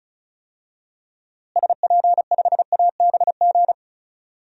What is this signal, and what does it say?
Morse code, heard as a keyed tone, about 36 wpm, 700 Hz. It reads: SP5ABG
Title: DXCC-100 Call Signs - 6 Characters - Repeated 2x - 35wpm